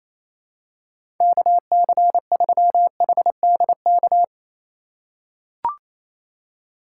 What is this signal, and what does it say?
Morse code, heard as a keyed tone, about 28 wpm, 700 Hz. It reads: KC3HDK
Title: USA Call Signs - 6 Characters - 3x Character Spacing - 28wpm